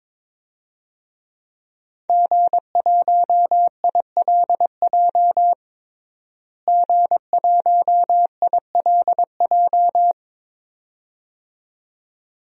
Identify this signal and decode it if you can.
Morse code, heard as a keyed tone, about 22 wpm, 700 Hz. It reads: G1ILJ G1ILJ
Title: DXCC-100 Call Signs - 5 Characters - Repeated 2x - 22wpm